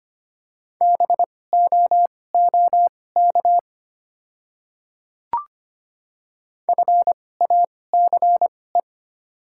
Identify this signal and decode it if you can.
Morse code, heard as a keyed tone, about 25 wpm, spaced as for 18 wpm, 700 Hz. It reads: BOOK FACE
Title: 2x Character Spacing - Top 300 Words - 25wpm